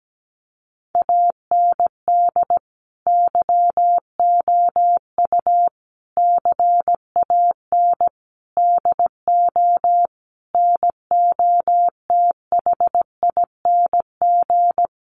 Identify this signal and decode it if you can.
Morse code, heard as a keyed tone, about 17 wpm, 700 Hz. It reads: AND YOU CAN DO NOTHING
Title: Sentences from Top 500 Words 17wpm